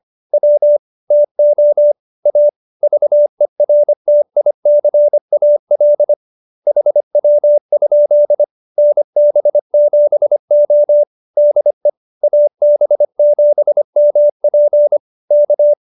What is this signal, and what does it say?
Morse code, heard as a keyed tone, about 25 wpm, 590 Hz. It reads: W TO A VERTICAL HW? NB7O DE AB7MP K
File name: QSO306 25